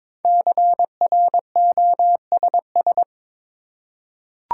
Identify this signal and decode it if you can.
Morse code, heard as a keyed tone, about 22 wpm, 700 Hz. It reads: CROSS
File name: 5 Letter Words - First 3 Letters Given - 22wpm